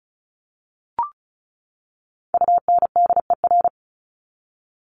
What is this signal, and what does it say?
Morse code, heard as a keyed tone, about 34 wpm, 700 Hz. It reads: UNDER
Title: Top 300 Words 35wpm